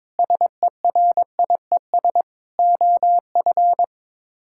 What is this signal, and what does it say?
Morse code, heard as a keyed tone, about 22 wpm, 700 Hz. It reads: SERIES OF
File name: practice / 22wpm / Sets of 7 Words - 22wpm